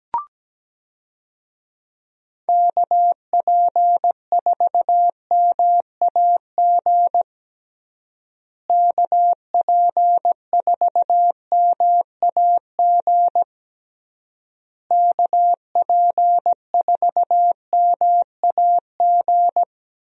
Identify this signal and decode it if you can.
Morse code, heard as a keyed tone, about 17 wpm, 700 Hz. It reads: KP4MAG KP4MAG KP4MAG
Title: USA Call Signs - 6 Characters - Repeated 3x - 17wpm